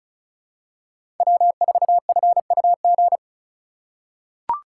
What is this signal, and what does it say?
Morse code, heard as a keyed tone, about 36 wpm, 700 Hz. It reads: W4FUG
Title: USA Call Signs - 5 Characters - 3x Character Spacing - 35wpm